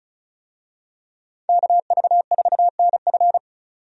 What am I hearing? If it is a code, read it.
Morse code, heard as a keyed tone, about 36 wpm, 700 Hz. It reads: KV4NF
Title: USA Call Signs - 5 Characters - 35wpm